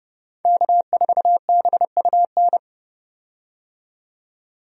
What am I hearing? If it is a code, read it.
Morse code, heard as a keyed tone, about 30 wpm, 700 Hz. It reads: K4BUN
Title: USA Call Signs - 5 Characters - 30wpm